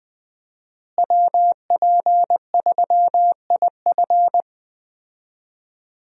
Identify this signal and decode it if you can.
Morse code, heard as a keyed tone, about 21 wpm, 700 Hz. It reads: WP3IF
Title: USA Call Signs - 5 Characters - Repeated 2x - 20wpm